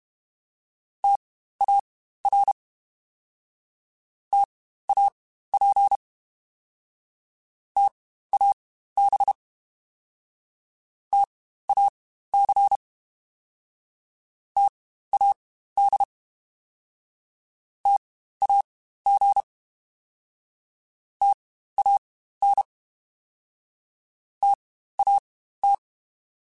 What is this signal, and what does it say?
Morse code, heard as a keyed tone, about 32 wpm, spaced as for 15 wpm, 800 Hz. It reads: TAR TAP TAB TAC TAD TAG TAN TAT
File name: Spaced Alphas / Spaced Alphas mp3s / Ts sp @30